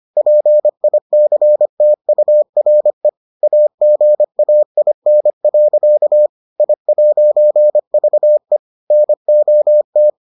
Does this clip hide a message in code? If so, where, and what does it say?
Yes: Morse code from 0.2 to 10.1 s, heard as a keyed tone, about 25 wpm, 600 Hz. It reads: PICTURE AGAIN. I'VE NOT